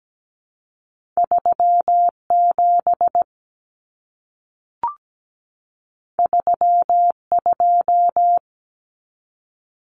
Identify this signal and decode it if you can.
Morse code, heard as a keyed tone, about 17 wpm, 700 Hz. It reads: 37 32
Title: Common Cut Numbers - 2-Digit 17wpm